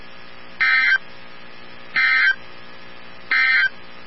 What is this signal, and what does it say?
Emergency Alert System (SAME) tones, heard as harsh buzzing bursts. It is heard from 0.5 to 3.8 s.
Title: NOAA Weather Radio 162.550